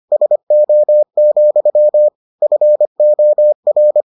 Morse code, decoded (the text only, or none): SO, FOR